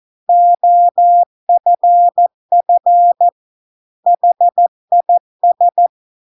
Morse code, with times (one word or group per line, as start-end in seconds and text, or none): 0.3-3.3 OFF
4.1-5.9 HIS